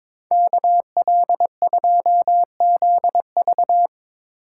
Morse code, heard as a keyed tone, about 25 wpm, 700 Hz. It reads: KL2ZV
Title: Speed Racing - USA Call Signs 22wpm